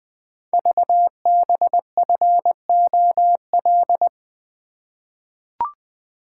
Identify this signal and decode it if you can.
Morse code, heard as a keyed tone, about 20 wpm, 700 Hz. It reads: VBFOL
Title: Code Groups - 5 Letters - 20wpm